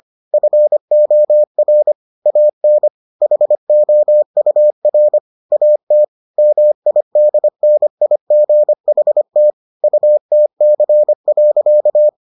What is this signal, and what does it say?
Morse code, heard as a keyed tone, about 25 wpm, 600 Hz. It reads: FOR AN HOUR AT MIDNIGHT UTC.